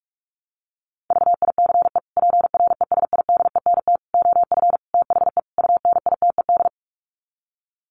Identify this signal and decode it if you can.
Morse code, heard as a keyed tone, about 44 wpm, 700 Hz. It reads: VICE PRESIDENT OF THE UNITED
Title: Sets of 5 Words Encore2 45wpm